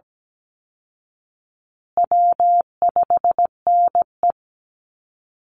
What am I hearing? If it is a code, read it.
Morse code, heard as a keyed tone, about 17 wpm, 700 Hz. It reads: W5NE
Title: Easy Call Signs Repeated 3x - 17wpm